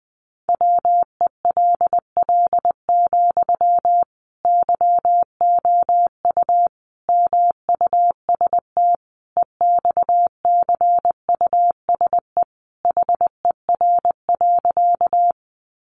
Morse code, heard as a keyed tone, about 20 wpm, 700 Hz. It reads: WELL, YOU MUST EXCUSE HER.
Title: Sentences from Top 700 Words 20wpm